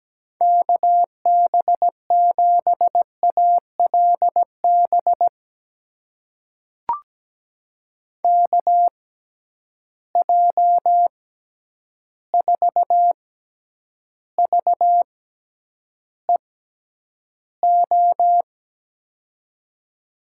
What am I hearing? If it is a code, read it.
Morse code, heard as a keyed tone, about 17 wpm, spaced as for 9 wpm, 700 Hz. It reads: KB7ALB KJ4VEO
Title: USA Call Signs - 6 Characters - 6x Character Spacing - 17wpm